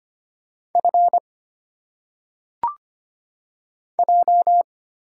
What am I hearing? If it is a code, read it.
Morse code, heard as a keyed tone, about 25 wpm, 700 Hz. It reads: F J
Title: Single Letters - Phonetic Pronunciation 25wpm